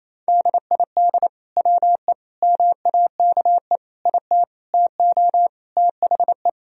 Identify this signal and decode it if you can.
Morse code, heard as a keyed tone, about 28 wpm, 700 Hz. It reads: DID WE MAKE IT TO THE
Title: Sentences from Top 1000 Words - Review Part 3 - 28wpm